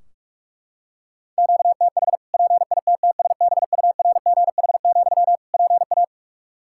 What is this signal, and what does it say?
Morse code, heard as a keyed tone, about 41 wpm, 700 Hz. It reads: QTH PITTSBURGH, PA
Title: Practice QSO - Exchange 1 - Beginning - 45wpm